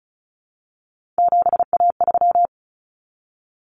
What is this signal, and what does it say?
Morse code, heard as a keyed tone, about 34 wpm, 700 Hz. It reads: 7A3
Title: Common Cut Numbers - 3-Digit 35wpm